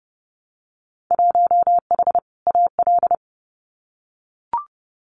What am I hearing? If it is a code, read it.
Morse code, heard as a keyed tone, about 31 wpm, 700 Hz. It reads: 1H AL
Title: Winter Field Day Exchanges 30wpm